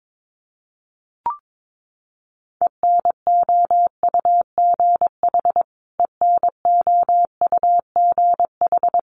Morse code, heard as a keyed tone, about 22 wpm, 700 Hz. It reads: ENOUGH ENOUGH
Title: Top 500 Words Repeated 22wpm